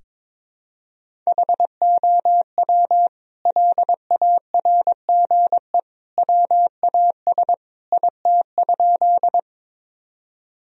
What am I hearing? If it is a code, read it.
Morse code, heard as a keyed tone, about 22 wpm, 700 Hz. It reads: HOW LARGE WAS IT?
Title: Sentences from Top 1000 Words - Review Part 1 - 22wpm